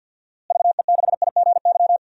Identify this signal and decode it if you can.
Morse code, heard as a keyed tone, about 52 wpm, 700 Hz. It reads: VE6IGY
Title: DXCC-100 Call Signs - 6 Characters - Repeated 2x - 50wpm